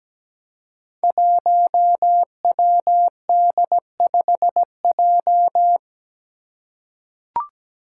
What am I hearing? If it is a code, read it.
Morse code, heard as a keyed tone, about 17 wpm, 700 Hz. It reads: 1WD5J
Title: Five Challenging Letter-Number Combinations 17wpm